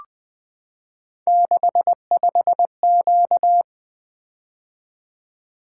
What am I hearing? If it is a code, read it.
Morse code, heard as a keyed tone, about 20 wpm, 700 Hz. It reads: 65Q
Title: Three Letter-Number Combinations - 20wpm